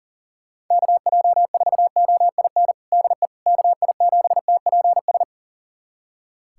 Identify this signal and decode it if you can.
Morse code, heard as a keyed tone, about 40 wpm, 700 Hz. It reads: KJ4OIN DE KI7TPS
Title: USA Callsign DE Callsign 40wpm